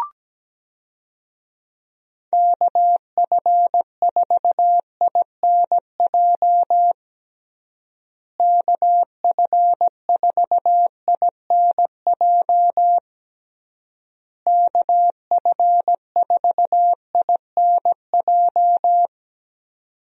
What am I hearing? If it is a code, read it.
Morse code, heard as a keyed tone, about 17 wpm, 700 Hz. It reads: KF4INJ KF4INJ KF4INJ
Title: USA Call Signs - 6 Characters - Repeated 3x - 17wpm